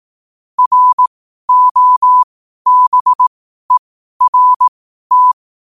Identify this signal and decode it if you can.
Morse code, heard as a keyed tone, about 18 wpm, spaced as for 12 wpm, 1000 Hz. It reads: ROBERT
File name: qotdfiles / 12wpm